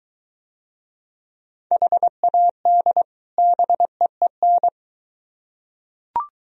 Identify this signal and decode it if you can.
Morse code, heard as a keyed tone, about 23 wpm, 700 Hz. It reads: HAD BEEN